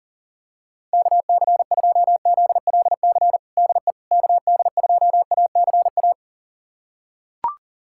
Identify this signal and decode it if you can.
Morse code, heard as a keyed tone, about 40 wpm, 700 Hz. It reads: KC2ZLC DE KD2ACA
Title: USA Callsign DE Callsign 40wpm